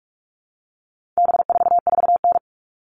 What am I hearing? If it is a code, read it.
Morse code, heard as a keyed tone, about 44 wpm, 700 Hz. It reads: B44N
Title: All Cut Numbers - 4-Digit 45wpm